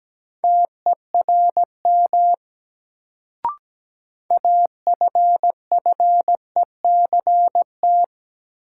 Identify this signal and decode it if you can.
Morse code, heard as a keyed tone, about 17 wpm, 700 Hz. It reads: TERM AFFECT
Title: Sentences from Top 900 Words - New Word Drill - 17wpm